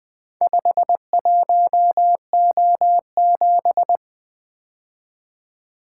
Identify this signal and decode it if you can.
Morse code, heard as a keyed tone, about 23 wpm, 700 Hz. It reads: 51O7
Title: Speed-Racing - Four Letter-Number Combinations - 20wpm